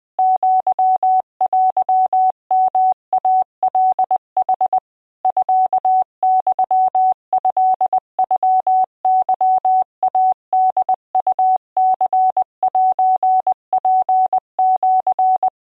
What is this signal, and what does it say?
Morse code (Japanese Wabun code), heard as a keyed tone, about 20 wpm, 750 Hz. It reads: アテヨイカヌ ミユトノケイホウニセツシ